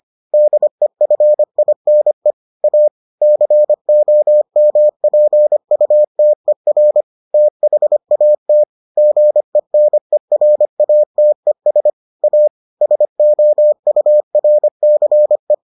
Morse code, heard as a keyed tone, about 25 wpm, 600 Hz. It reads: DEFINE A COMPUTER THAT GENERATES A SOURCE